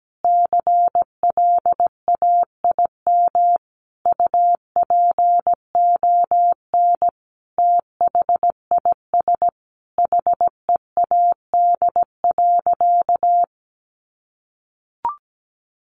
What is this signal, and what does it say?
Morse code, heard as a keyed tone, about 17 wpm, 700 Hz. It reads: CLAIM UPON THIS HEAD.
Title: Sentences from Top 1500 Words 17wpm